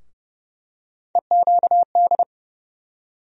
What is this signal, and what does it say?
Morse code, heard as a keyed tone, about 30 wpm, 700 Hz. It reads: EQD
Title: USA Call Sign Postfixes - 3 Letters 30wpm